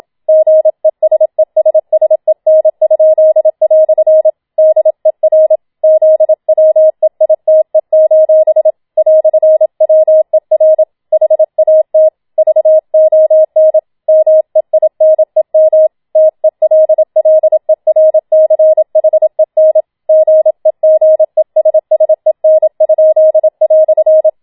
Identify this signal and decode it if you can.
Morse code, heard as a keyed tone, about 27 wpm, 630 Hz. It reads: GESESSEN?" DER ZWEITE: "WER HAT VON MEINEM TELLERCHEN GEGESSEN?"